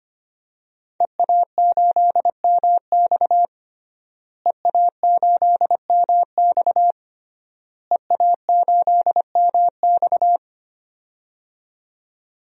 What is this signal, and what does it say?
Morse code, heard as a keyed tone, about 25 wpm, 700 Hz. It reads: EA8MX EA8MX EA8MX
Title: DXCC-100 Call Signs - 5 Characters - Repeated 3x - 25wpm